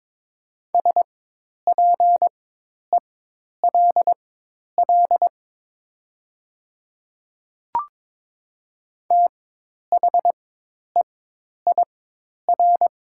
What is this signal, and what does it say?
Morse code, heard as a keyed tone, about 22 wpm, spaced as for 10 wpm, 700 Hz. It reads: SPELL THEIR
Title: 4x Character Spacing - Top 300 Words - 22wpm